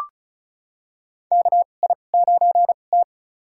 Morse code, heard as a keyed tone, about 36 wpm, spaced as for 25 wpm, 700 Hz. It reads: KI9T
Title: USA Call Signs - 4 Characters - 2x Character Spacing - 35wpm